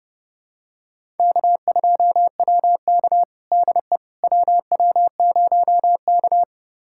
Morse code, heard as a keyed tone, about 30 wpm, 700 Hz. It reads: K2WK DE WW0K
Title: USA Callsign DE Callsign - 4 Characters - 30wpm